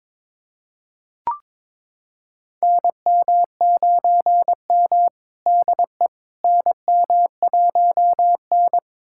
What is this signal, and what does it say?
Morse code, heard as a keyed tone, about 22 wpm, 700 Hz. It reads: NM9M DE NM1N
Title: USA Callsign DE Callsign - 4 Characters - 22wpm